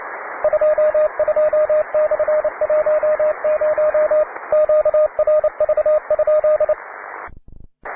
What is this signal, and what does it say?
Morse code, heard as a keyed tone, about 27 wpm, 610 Hz, with radio noise. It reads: 22/10 QRV?
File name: wae / cw advanced / w3bp cw